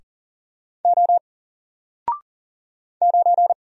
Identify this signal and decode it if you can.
Morse code, heard as a keyed tone, about 40 wpm, 700 Hz. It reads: O 9
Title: Single Character (Letter:Number:Punctuation:Prosign) Alt 40wpm